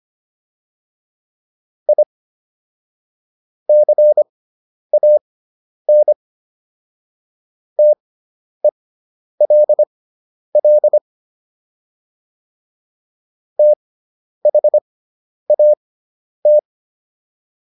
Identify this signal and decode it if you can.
Morse code, heard as a keyed tone, about 25 wpm, spaced as for 10 wpm, 600 Hz. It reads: I CAN TELL THAT